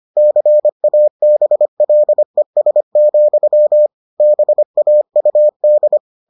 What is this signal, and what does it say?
Morse code, heard as a keyed tone, about 25 wpm, 600 Hz. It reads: CABLES, BAUD